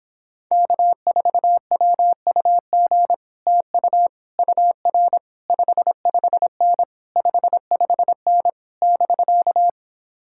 Morse code, heard as a keyed tone, about 26 wpm, 700 Hz. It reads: K4WUG TU UR 55N 55N <BK>